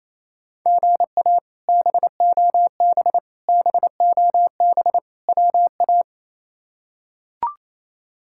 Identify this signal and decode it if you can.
Morse code, heard as a keyed tone, about 28 wpm, 700 Hz. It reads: GA BOB BOB WA
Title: K1USN SST QSO Practice - 28wpm